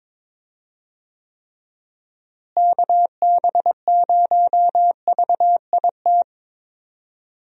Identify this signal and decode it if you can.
Morse code, heard as a keyed tone, about 22 wpm, 700 Hz. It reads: KB0VIT
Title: USA Call Signs - 6 Characters - Repeated 3x - 22wpm